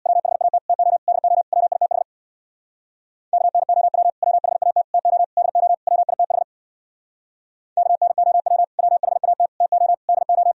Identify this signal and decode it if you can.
Morse code, heard as a keyed tone, about 63 wpm, 700 Hz. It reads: WHAT TO DO WITH KNOW WHAT TO DO WITH KNOW WHAT TO DO